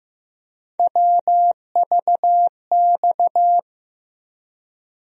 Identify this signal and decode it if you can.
Morse code, heard as a keyed tone, about 15 wpm, 700 Hz. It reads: WVX
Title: USA Call Sign Postfixes - 3 Letters 15wpm